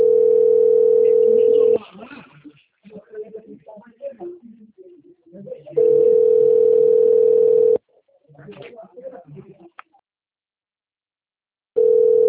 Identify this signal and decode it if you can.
Ringback tone (North America), 440 + 480 Hz, with speech.
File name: rg-1600-unknown- - - .30502